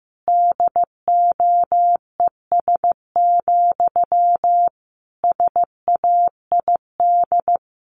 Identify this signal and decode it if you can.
Morse code, heard as a keyed tone, about 15 wpm, 700 Hz. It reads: DOES, SAID
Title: Sentences from Top 1500 Words 15wpm